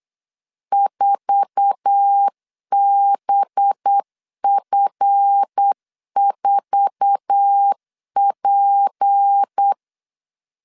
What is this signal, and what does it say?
Morse code, heard as a keyed tone, about 8 wpm, 800 Hz. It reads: 4BF4P
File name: LEZIONI / 8 YJ45 / 8 YJ4 8wpm 3246